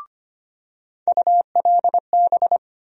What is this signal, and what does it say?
Morse code, heard as a keyed tone, about 25 wpm, 700 Hz. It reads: ULB
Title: USA Call Sign Postfixes - 3 Letters 25wpm